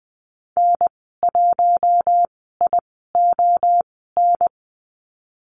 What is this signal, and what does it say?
Morse code, heard as a keyed tone, about 20 wpm, spaced as for 14 wpm, 700 Hz. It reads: N1ION
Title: 042 Lesson 8g - Number 1 - Call Signs 20wpm